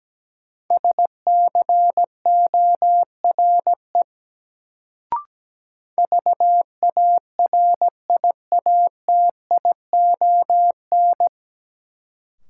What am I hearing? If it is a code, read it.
Morse code, heard as a keyed tone, about 17 wpm, 700 Hz. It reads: SCORE VARIATION
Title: Sentences from Top 1600 Words - New Word Drill - 17wpm